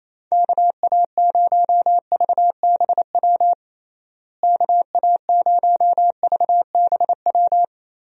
Morse code, heard as a keyed tone, about 28 wpm, 700 Hz. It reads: KA0VBW KA0VBW
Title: USA Call Signs - 6 Characters - Repeated 2x - 28wpm